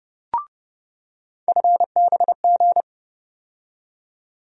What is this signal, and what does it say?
Morse code, heard as a keyed tone, about 30 wpm, 700 Hz. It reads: FBG
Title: USA Call Sign Postfixes - 3 Letters 30wpm